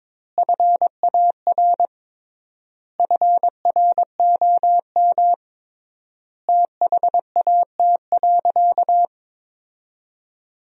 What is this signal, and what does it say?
Morse code, heard as a keyed tone, about 22 wpm, 700 Hz. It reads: FAR FROM THAT.
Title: 3x Word-Spacing - Sentences from Top 300 Words - 22wpm